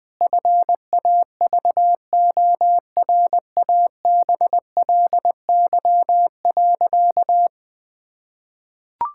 Morse code, heard as a keyed tone, about 21 wpm, 700 Hz. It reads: FAVORABLY.
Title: Pangram Challenge v2 - 20wpm